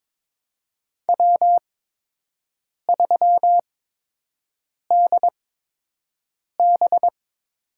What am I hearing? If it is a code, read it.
Morse code, heard as a keyed tone, about 22 wpm, spaced as for 6 wpm, 700 Hz. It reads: W3DB
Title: USA Call Signs - 4 Characters - 8x Character Spacing - 22wpm